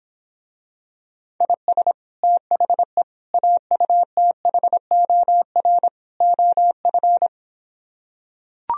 Morse code, heard as a keyed tone, about 26 wpm, 700 Hz. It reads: IS THE AUTHOR OF